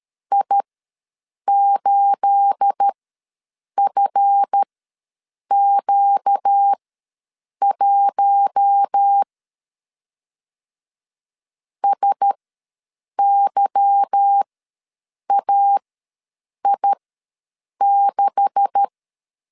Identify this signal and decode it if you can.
Morse code, heard as a keyed tone, about 13 wpm, spaced as for 7 wpm, 800 Hz. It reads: I8FQ1 SYAI6
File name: LEZIONI / 9 286 ws 12wpm 3351